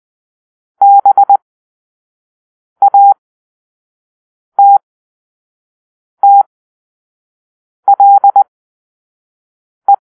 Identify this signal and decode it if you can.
Morse code, heard as a keyed tone, about 20 wpm, spaced as for 5 wpm, 800 Hz. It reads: BATTLE
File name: arrl / Grp-4 Drill-1